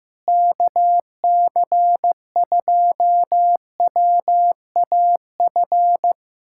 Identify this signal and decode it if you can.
Morse code, heard as a keyed tone, about 15 wpm, 700 Hz. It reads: KC2WAF
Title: USA Callsign DE Callsign 15wpm